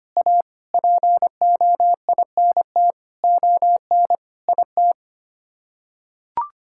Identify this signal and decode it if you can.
Morse code, heard as a keyed tone, about 25 wpm, 700 Hz. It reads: A POINT ON IT
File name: Sets of 9 Words - 25wpm